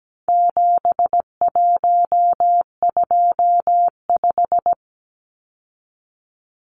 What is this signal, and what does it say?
Morse code, heard as a keyed tone, about 17 wpm, 700 Hz. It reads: 7125
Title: Numbers - 4-Digit 17wpm